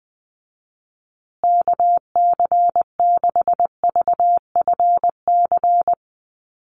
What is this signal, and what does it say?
Morse code, heard as a keyed tone, about 20 wpm, 700 Hz. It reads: KC6VFC
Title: USA Call Signs Repeated 2x 20wpm